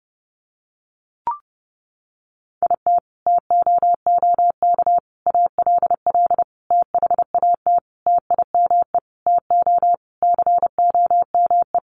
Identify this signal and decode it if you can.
Morse code, heard as a keyed tone, about 30 wpm, 700 Hz. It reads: IT TOOK ALL THAT TIME TO COME
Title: Sentences from Top 300 Words 30wpm